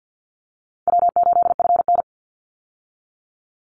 Morse code, heard as a keyed tone, about 50 wpm, 700 Hz. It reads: W8FN
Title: Easy Call Signs Repeated 3x - 50wpm